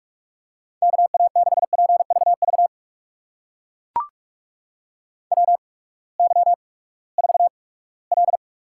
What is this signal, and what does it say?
Morse code, heard as a keyed tone, about 44 wpm, 690 Hz. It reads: K A 6 P V V W Y 4 L
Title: USA Call Signs - 8x Character Spacing - 45wpm